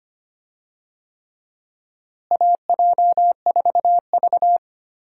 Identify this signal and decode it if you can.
Morse code, heard as a keyed tone, about 25 wpm, 700 Hz. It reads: AJ4V
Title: USA Call Signs - 4 Characters - Repeated 3x - 25wpm